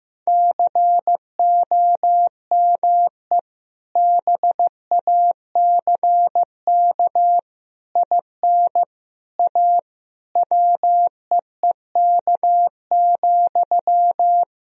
Morse code, heard as a keyed tone, about 15 wpm, 700 Hz. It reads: COME BACK IN A WEEK,
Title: Sentences from Top 1000 Words - Review Part 3 - 15wpm